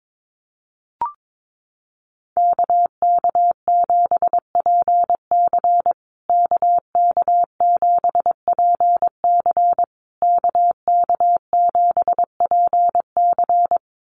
Morse code, heard as a keyed tone, about 22 wpm, 690 Hz. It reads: KK7PC KK7PC KK7PC
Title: USA Call Signs Repeated 3x 22wpm